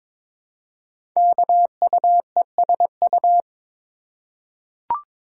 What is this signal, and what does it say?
Morse code, heard as a keyed tone, about 22 wpm, 690 Hz. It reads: KUESU
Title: Code Groups - 5 Letter:Numbers - 22wpm